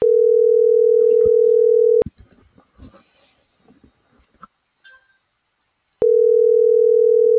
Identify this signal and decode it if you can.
Ringback tone (North America), 440 + 480 Hz.